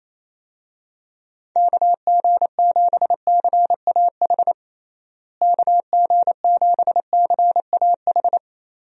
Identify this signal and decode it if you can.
Morse code, heard as a keyed tone, about 28 wpm, 700 Hz. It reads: KG7CAH KG7CAH
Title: USA Call Signs - 6 Characters - Repeated 2x - 28wpm